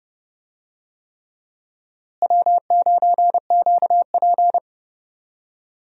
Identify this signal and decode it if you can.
Morse code, heard as a keyed tone, about 30 wpm, 700 Hz. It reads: W9QP
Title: USA Call Signs - 4 Characters - Repeated 3x - 30wpm